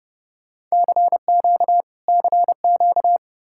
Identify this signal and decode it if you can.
Morse code, heard as a keyed tone, about 35 wpm, 700 Hz. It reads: CQ CQ